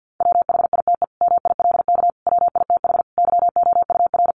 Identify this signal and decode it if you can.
Morse code, heard as a keyed tone, about 50 wpm, 700 Hz. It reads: WHITE MILK WITH YOUR
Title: Sentences from Top 1500 Words 50wpm